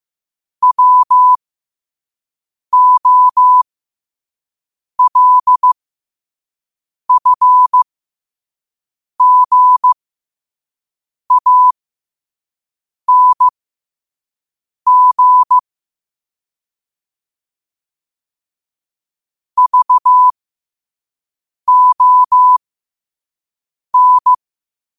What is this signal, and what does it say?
Morse code, heard as a keyed tone, about 15 wpm, spaced as for 5 wpm, 1000 Hz. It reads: WOLFGANG VON